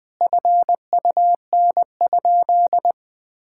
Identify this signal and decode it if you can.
Morse code, heard as a keyed tone, about 21 wpm, 700 Hz. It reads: FUN?
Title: Sentences with 4-Word Phrases - 20wpm